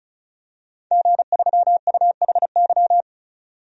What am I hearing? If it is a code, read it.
Morse code, heard as a keyed tone, about 34 wpm, 700 Hz. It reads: G3UHY
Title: DXCC-100 Call Signs - 5 Characters - 35wpm